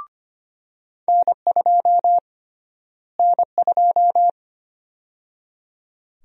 Morse code, heard as a keyed tone, about 25 wpm, 700 Hz. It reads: N2 N2
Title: USA Call Sign Prefixes - Repeated 2x - 25wpm